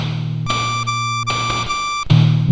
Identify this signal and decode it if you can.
There is no code.